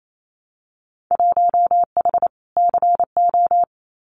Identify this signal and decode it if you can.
Morse code, heard as a keyed tone, about 28 wpm, 700 Hz. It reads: 1H CO
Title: Winter Field Day Exchanges 28wpm